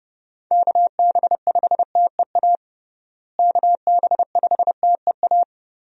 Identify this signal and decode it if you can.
Morse code, heard as a keyed tone, about 29 wpm, 700 Hz. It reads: KB5TEA KB5TEA
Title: USA Call Signs - 6 Characters - Repeated 3x - 30wpm